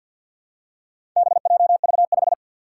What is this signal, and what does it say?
Morse code, heard as a keyed tone, about 46 wpm, 700 Hz. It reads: BJV5
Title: Four Challenging Letter-Number Combinations 50wpm